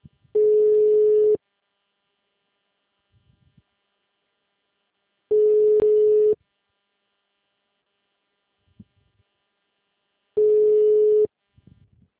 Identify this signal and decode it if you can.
Ringback tone (Europe, 425 Hz).